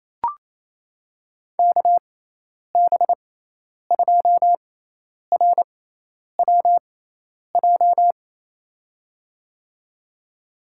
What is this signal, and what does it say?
Morse code, heard as a keyed tone, about 28 wpm, spaced as for 10 wpm, 700 Hz. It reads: KB2RWJ
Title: USA Call Signs - 6x Character Spacing - 28wpm